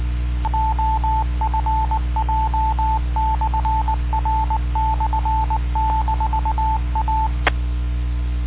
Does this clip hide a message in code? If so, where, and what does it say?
Yes: Morse code from 0.5 to 7.3 s, heard as a keyed tone, about 18 wpm, 880 Hz, with radio noise. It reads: OFJ/R/-A